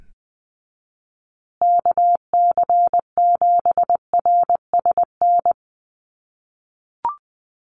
Morse code, heard as a keyed tone, about 20 wpm, 700 Hz. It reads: KC7RSN
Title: USA Call Signs Repeated 2x 20wpm